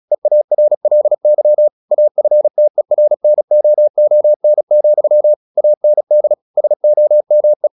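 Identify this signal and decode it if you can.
Morse code, heard as a keyed tone, about 36 wpm, 600 Hz. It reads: EARLY AFTERNOON, AND SOME
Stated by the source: Morse practice at 36 wpm, text of Ken Follett's novel Never, from CW med Gnister.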